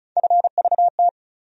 Morse code, heard as a keyed tone, about 36 wpm, 700 Hz. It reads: FVT
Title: USA Call Sign Postfixes - 3 Letters 35wpm